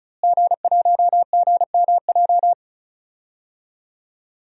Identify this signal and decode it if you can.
Morse code, heard as a keyed tone, about 36 wpm, 700 Hz. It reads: G1GMJ
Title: DXCC-100 Call Signs - 5 Characters - Repeated 2x - 35wpm